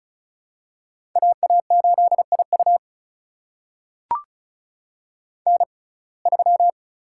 Morse code, heard as a keyed tone, about 36 wpm, spaced as for 26 wpm, 700 Hz. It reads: AA8IU N3
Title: USA Call Signs - 5 Characters - 6x Character Spacing - 35wpm